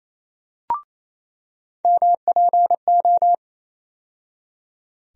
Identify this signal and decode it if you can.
Morse code, heard as a keyed tone, about 28 wpm, 700 Hz. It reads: MPO
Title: USA Call Sign Postfixes - 3 Letters 28wpm